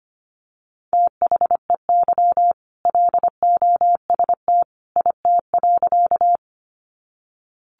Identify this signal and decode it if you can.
Morse code, heard as a keyed tone, about 25 wpm, 700 Hz. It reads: THEY LOST IT.
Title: Sentences from Top 1400 Words 25wpm